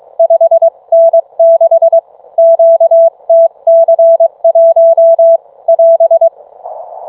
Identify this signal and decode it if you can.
Morse code, heard as a keyed tone, about 22 wpm, 660 Hz. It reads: 5N6 QTC1 <AS>